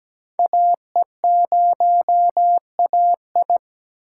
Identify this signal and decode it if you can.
Morse code, heard as a keyed tone, about 20 wpm, 700 Hz. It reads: AE0AI
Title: Speed Racing - USA 2x2 Call Signs 17wpm